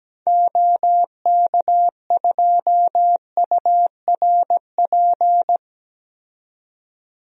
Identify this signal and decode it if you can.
Morse code, heard as a keyed tone, about 17 wpm, 700 Hz. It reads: OK2URP
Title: DXCC-100 Call Signs - 6 Characters - 17wpm